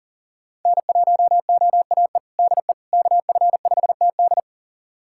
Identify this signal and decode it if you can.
Morse code, heard as a keyed tone, about 40 wpm, 700 Hz. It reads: N1OAE DE KF5TD
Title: USA Callsign DE Callsign - 5 Characters - 40wpm